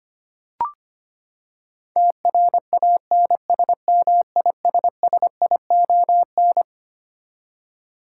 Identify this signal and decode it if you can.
Morse code, heard as a keyed tone, about 25 wpm, 700 Hz. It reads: TRANSMISSION
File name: Twelve Letter Words - 25wpm